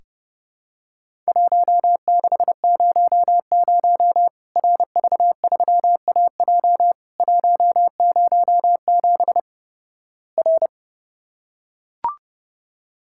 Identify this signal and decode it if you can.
Morse code, heard as a keyed tone, about 32 wpm, 700 Hz. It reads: 1600 RV3AJ 107 R
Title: Worked All Europe DX Contest - QTC Exchange - 30wpm